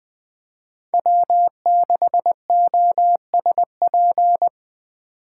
Morse code, heard as a keyed tone, about 19 wpm, 700 Hz. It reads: W6OSP
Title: K1USN SST QSO Practice - 20wpm